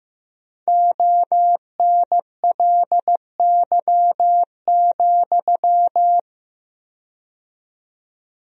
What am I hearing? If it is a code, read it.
Morse code, heard as a keyed tone, about 15 wpm, 700 Hz. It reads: ONLY,
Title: 4x Word-Spacing - Sentences from Top 300 Words - 15wpm